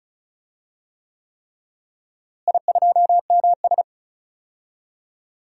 Morse code, heard as a keyed tone, about 34 wpm, 700 Hz. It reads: I2MS
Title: DXCC-100 Call Signs - 4 Characters - Repeated 3x - 35wpm